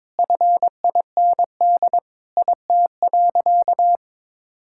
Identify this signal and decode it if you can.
Morse code, heard as a keyed tone, about 22 wpm, 700 Hz. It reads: FIND IT.
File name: practice / 22wpm / Sentences with 4-Word Phrases - 22wpm